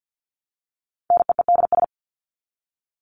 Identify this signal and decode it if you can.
Morse code, heard as a keyed tone, about 50 wpm, 700 Hz. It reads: NEEDS
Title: ES:S Suffix - Top 100 Words with Base Words 50wpm